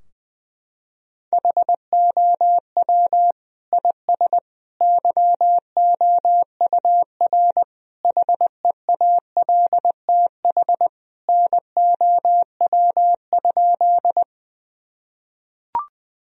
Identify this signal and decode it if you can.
Morse code, heard as a keyed tone, about 20 wpm, 700 Hz. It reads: HOW IS YOUR HEALTH NOW?
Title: Sentences from Top 1000 Words - Review Part 3 - 20wpm